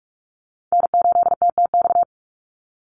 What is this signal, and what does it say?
Morse code, heard as a keyed tone, about 44 wpm, 700 Hz. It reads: N8TTX
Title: USA Call Signs 45wpm